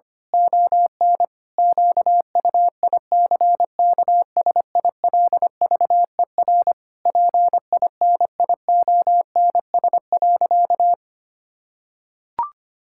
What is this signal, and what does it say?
Morse code, heard as a keyed tone, about 25 wpm, 700 Hz. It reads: ON QUICKSILVER PINIONS.
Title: Pangram Challenge v2 - 25wpm